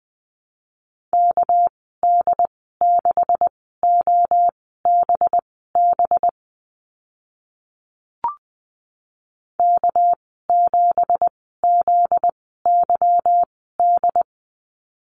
Morse code, heard as a keyed tone, about 20 wpm, spaced as for 15 wpm, 700 Hz. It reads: KD6OBB K7ZYD
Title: 245 Lesson 39g - Letter z - Call Signs 20wpm